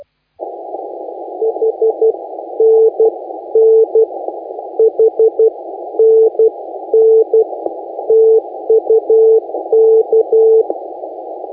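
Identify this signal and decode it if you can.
Morse code, heard as a keyed tone, about 12 wpm, 440 Hz. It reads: H N N H N N TUK
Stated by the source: Morse contest audio recorded by LA4C during the SAC CW 2018 contest.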